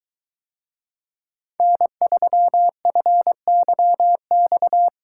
Morse code, heard as a keyed tone, about 22 wpm, 700 Hz. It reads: N3FYX